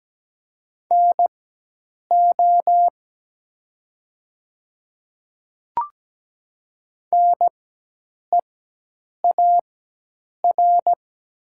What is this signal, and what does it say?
Morse code, heard as a keyed tone, about 17 wpm, spaced as for 8 wpm, 700 Hz. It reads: NO NEAR
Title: 4x Character Spacing - Top 300 Words - 17wpm